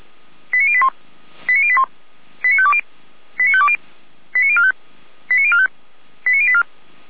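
Five-tone selective call (ZVEI), heard as quick runs of five tones, tones of about 70 ms. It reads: 89081 89081 89520 89520 89045 89045 89084
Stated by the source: ZVEI five-tone call 89512 received by SDR.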